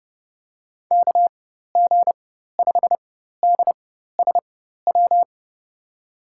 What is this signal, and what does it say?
Morse code, heard as a keyed tone, about 30 wpm, spaced as for 14 wpm, 700 Hz. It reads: KG5DSW
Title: USA Call Signs - 6 Characters - 4x Character Spacing - 30wpm